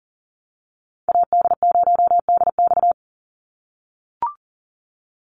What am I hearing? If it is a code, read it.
Morse code, heard as a keyed tone, about 40 wpm, 700 Hz. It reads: AD0DX
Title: Call Signs - 40wpm